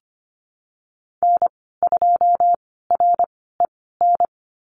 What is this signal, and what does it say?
Morse code, heard as a keyed tone, about 25 wpm, spaced as for 16 wpm, 700 Hz. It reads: N2REN
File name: Lesson 15g - Number 5 - Call Signs 25wpm